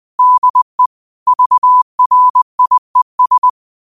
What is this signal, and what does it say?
Morse code, heard as a keyed tone, about 20 wpm, 1000 Hz. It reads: DE VRIES
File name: qotdfiles / 20wpm